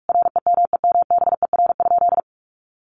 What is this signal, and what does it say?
Morse code, heard as a keyed tone, about 44 wpm, 700 Hz. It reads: REMEMBER?
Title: Sentences from Top 400 Words 45wpm